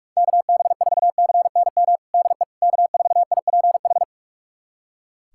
Morse code, heard as a keyed tone, about 46 wpm, 700 Hz. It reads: KB4CNM DE K4IPH
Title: USA Callsign DE Callsign 45wpm